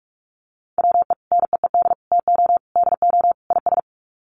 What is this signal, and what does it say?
Morse code, heard as a keyed tone, about 44 wpm, 700 Hz. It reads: WE NEED TO DO IS
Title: Sets of 5 Words Encore2 45wpm